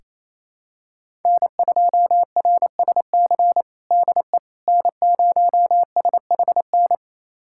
Morse code, heard as a keyed tone, about 28 wpm, 700 Hz. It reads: N2RSC DE N0SHN
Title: USA Callsign DE Callsign - 5 Characters - 28wpm